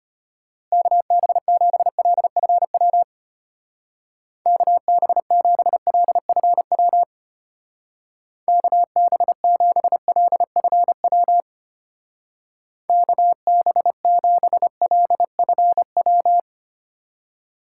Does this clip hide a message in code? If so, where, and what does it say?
Yes: Morse code from 0.7 to 16.4 s, heard as a keyed tone, about 32 wpm, 700 Hz. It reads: KB7LFW KB7LFW KB7LFW KB7LFW